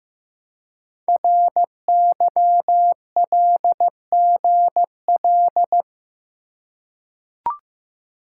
Morse code, heard as a keyed tone, about 15 wpm, 690 Hz. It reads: RYLGL
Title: Code Groups - 5 Letters - 15wpm